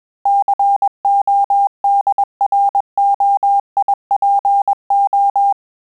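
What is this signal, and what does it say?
Morse code, heard as a keyed tone, about 21 wpm, 800 Hz. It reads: CODROIPO
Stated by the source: Morse QSO practice at 20 words per minute, in Italian and English.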